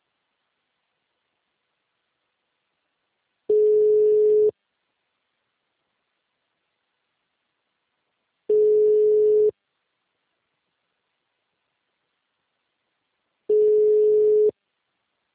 Ringback tone (Europe, 425 Hz).
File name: out- -2006- - - .114